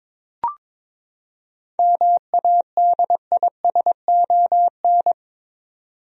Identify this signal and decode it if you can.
Morse code, heard as a keyed tone, about 21 wpm, 700 Hz. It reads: MADISON
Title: USA Common Names - 22wpm